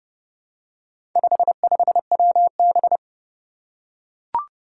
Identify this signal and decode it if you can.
Morse code, heard as a keyed tone, about 32 wpm, 700 Hz. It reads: 55WB
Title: Four Challenging Letter-Number Combinations 30wpm